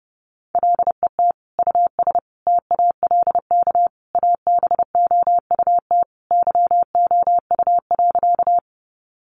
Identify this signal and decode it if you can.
Morse code, heard as a keyed tone, about 30 wpm, 700 Hz. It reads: LET US TALK ABOUT YOU.
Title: Sentences from Top 400 Words 30wpm